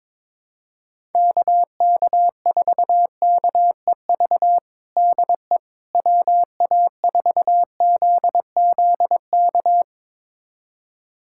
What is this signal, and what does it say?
Morse code, heard as a keyed tone, about 22 wpm, 700 Hz. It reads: KK4KEV DE WA4ZZK
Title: USA Callsign DE Callsign - 6 Characters - 22wpm